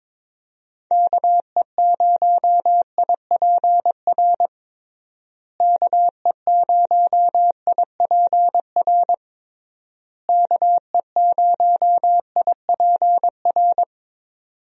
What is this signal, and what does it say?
Morse code, heard as a keyed tone, about 22 wpm, 700 Hz. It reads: KE0IPR KE0IPR KE0IPR
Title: USA Call Signs - 6 Characters - Repeated 3x - 22wpm